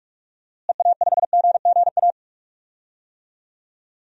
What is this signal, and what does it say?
Morse code, heard as a keyed tone, about 46 wpm, 700 Hz. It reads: EA5GGA
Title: DXCC-100 Call Signs - 6 Characters - Repeated 3x - 45wpm